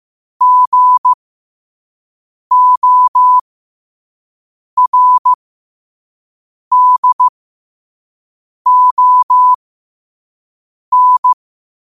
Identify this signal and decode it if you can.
Morse code, heard as a keyed tone, about 15 wpm, spaced as for 5 wpm, 1000 Hz. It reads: GORDON
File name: qotdfiles / 5wpm